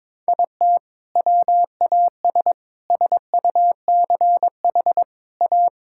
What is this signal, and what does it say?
Morse code, heard as a keyed tone, about 22 wpm, 700 Hz. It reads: IT WAS SUCH A
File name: practice / 22wpm / Sentences with 2-Word Phrases - 22wpm